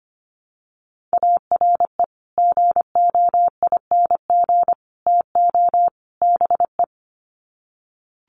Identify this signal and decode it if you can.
Morse code, heard as a keyed tone, about 25 wpm, 700 Hz. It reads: ARE GOING TO BE
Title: Sets of 4 Words 25wpm